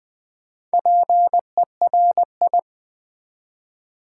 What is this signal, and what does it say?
Morse code, heard as a keyed tone, about 20 wpm, 700 Hz. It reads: PERI
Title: Build A Word - 2 Letters At A Time - 6-Letter Words - 20wpm